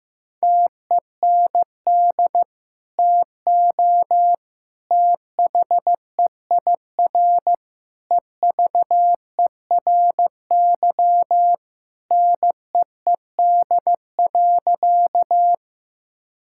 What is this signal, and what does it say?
Morse code, heard as a keyed tone, about 15 wpm, 700 Hz. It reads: TEND TO THEIR EVERY NEED.
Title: Sentences from Top 1000 Words - Review Part 5 - 15wpm